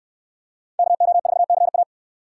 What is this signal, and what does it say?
Morse code, heard as a keyed tone, about 68 wpm, 700 Hz. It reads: KO4CA